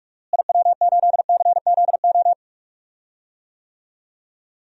Morse code, heard as a keyed tone, about 46 wpm, 700 Hz. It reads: IW8CZO
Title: DXCC-100 Call Signs - 6 Characters - Repeated 2x - 45wpm